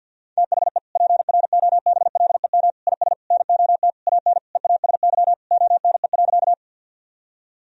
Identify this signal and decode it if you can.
Morse code, heard as a keyed tone, about 49 wpm, 700 Hz. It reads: THE PROBLEM IS NOT AN EASY ONE.
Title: Sentences from Top 1000 Words - Review Part 2 - 50wpm